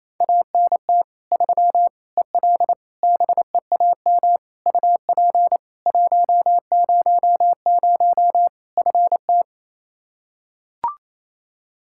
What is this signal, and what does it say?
Morse code, heard as a keyed tone, about 28 wpm, 700 Hz. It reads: ANT 3 EL BEAM UP 100 FT
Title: K1USN SST QSO Practice 28wpm